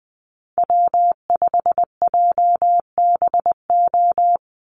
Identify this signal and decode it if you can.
Morse code, heard as a keyed tone, about 20 wpm, 700 Hz. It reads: W5JBO
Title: Call Signs Repeated 3x - 20wpm